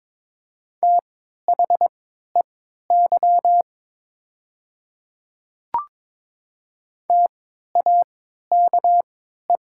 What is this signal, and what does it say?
Morse code, heard as a keyed tone, about 22 wpm, spaced as for 12 wpm, 690 Hz. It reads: THEY TAKE
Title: 3x Character Spacing - Top 200 Words - 22wpm